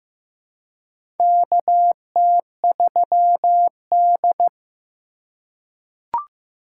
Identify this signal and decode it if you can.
Morse code, heard as a keyed tone, about 15 wpm, 700 Hz. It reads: KT3D
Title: USA Call Signs - 4x Character Spacing - 15wpm